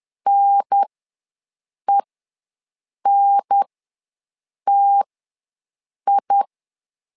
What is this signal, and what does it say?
Morse code, heard as a keyed tone, about 10 wpm, spaced as for 6 wpm, 800 Hz. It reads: NENTI
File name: 2 UHDN ws 10wpm